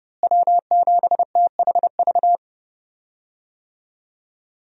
Morse code, heard as a keyed tone, about 30 wpm, 700 Hz. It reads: W7THV
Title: USA Call Signs - 5 Characters - Repeated 2x - 30wpm